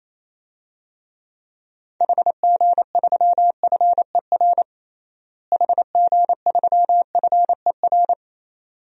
Morse code, heard as a keyed tone, about 28 wpm, 700 Hz. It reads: HG3FER HG3FER
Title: DXCC-100 Call Signs - 6 Characters - Repeated 2x - 28wpm